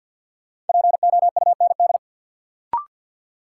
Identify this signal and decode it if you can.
Morse code, heard as a keyed tone, about 53 wpm, 700 Hz. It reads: POUND
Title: Sentences from Top 1000 Words - Review Part 2 - New Word Drill - 50wpm